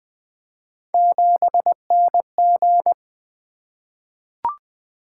Morse code, heard as a keyed tone, about 20 wpm, 700 Hz. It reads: 7NG
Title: Three Letter-Number Combinations - 20wpm